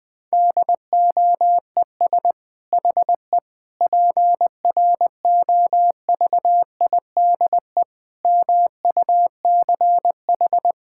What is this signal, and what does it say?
Morse code, heard as a keyed tone, about 21 wpm, 700 Hz. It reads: DOES HE PROVIDE MUCH
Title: Sentences from Top 1000 Words - Review Part 6 - 20wpm